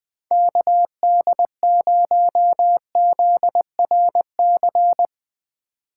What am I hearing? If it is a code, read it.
Morse code, heard as a keyed tone, about 20 wpm, 700 Hz. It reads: KD0ZRC
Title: USA Call Signs - 6 Characters - 8x Character Spacing - 20wpm